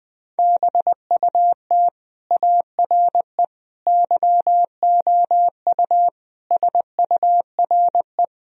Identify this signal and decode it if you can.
Morse code, heard as a keyed tone, about 20 wpm, 700 Hz. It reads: BUT ARE YOU SURE
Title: Sentences from Top 1000 Words - Review Part 1 - 20wpm